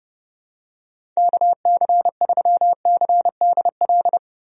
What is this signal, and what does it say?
Morse code, heard as a keyed tone, about 30 wpm, 700 Hz. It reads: KC3CDL
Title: USA Call Signs - 6 Characters - 30wpm